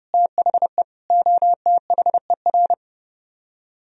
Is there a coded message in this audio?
Yes: Morse code from 0.1 to 2.7 s, heard as a keyed tone, about 26 wpm, 700 Hz. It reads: THE OTHER